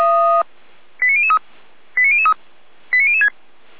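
Five-tone selective call (ZVEI), heard as quick runs of five tones, tones of about 80 ms. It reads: 89003 89003 89006
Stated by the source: ZVEI five-tone selective call received by SDR.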